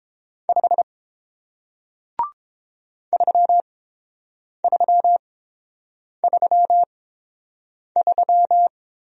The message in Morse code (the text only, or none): 5 3333